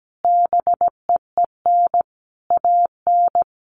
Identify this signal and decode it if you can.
Morse code, heard as a keyed tone, about 17 wpm, 700 Hz. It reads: BEEN AN
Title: Sentences from Top 1100 Words 17wpm